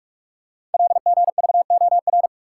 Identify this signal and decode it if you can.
Morse code, heard as a keyed tone, about 44 wpm, 700 Hz. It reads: LGVOR
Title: Code Groups - 5 Letters - 45wpm